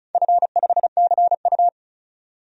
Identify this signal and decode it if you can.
Morse code, heard as a keyed tone, about 36 wpm, 700 Hz. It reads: F5CU
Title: DXCC-100 Call Signs - 4 Characters - 35wpm